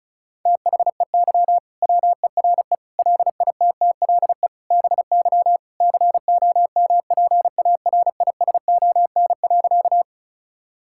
Morse code, heard as a keyed tone, about 34 wpm, 700 Hz. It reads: THEY WERE LITTLE BY COMPARISON.
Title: Sentences from Top 1600 Words 35wpm